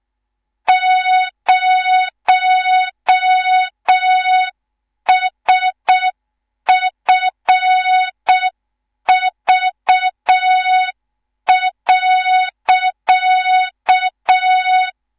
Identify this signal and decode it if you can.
Morse code, heard as a keyed tone, about 6 wpm, 3700 Hz. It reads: 0SFV.
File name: Morze-HA2MN / hang / 18 IRASJELEK2